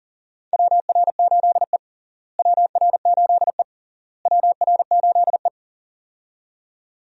Morse code, heard as a keyed tone, about 40 wpm, 700 Hz. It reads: WR8E WR8E WR8E
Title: USA Call Signs - 4 Characters - Repeated 3x - 40wpm